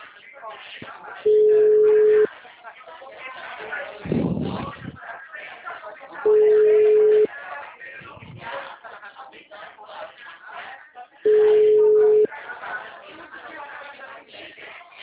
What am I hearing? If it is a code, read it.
Ringback tone (Europe, 425 Hz), with speech.